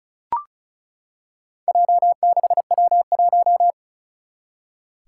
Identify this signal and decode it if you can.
Morse code, heard as a keyed tone, about 36 wpm, 700 Hz. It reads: J6W1
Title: Four Challenging Letter-Number Combinations 35wpm